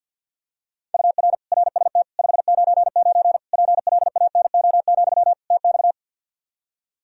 Morse code, heard as a keyed tone, about 49 wpm, 700 Hz. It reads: UR RST 599 PLANO, TX
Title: K1USN SST QSO Practice 50wpm